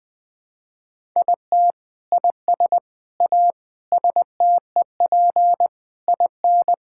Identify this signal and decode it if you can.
Morse code, heard as a keyed tone, about 20 wpm, 700 Hz. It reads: IT IS A STEP IN
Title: Sentences with 8-Word Phrases - 20wpm